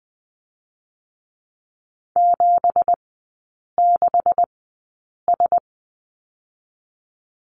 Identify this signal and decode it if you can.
Morse code, heard as a keyed tone, about 20 wpm, spaced as for 8 wpm, 700 Hz. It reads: 76S
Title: 166 Lesson 27e - Number 7 - Three Characters 20wpm